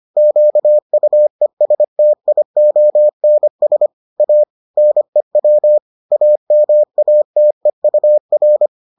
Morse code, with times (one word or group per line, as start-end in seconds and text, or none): 0.2-3.9 QUESTIONS
4.2-4.4 A
4.8-5.8 NEW
6.1-8.7 AMATEUR